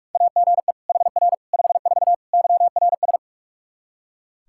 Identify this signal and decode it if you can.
Morse code, heard as a keyed tone, about 46 wpm, 700 Hz. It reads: AGE HR 54 YRS
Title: Practice QSO - Exchange 3 - More Info - 45wpm